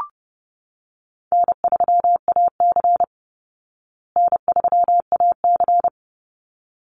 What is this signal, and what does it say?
Morse code, heard as a keyed tone, about 30 wpm, 700 Hz. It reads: N3AC N3AC
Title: Easy Call Signs Repeated 2x - 30wpm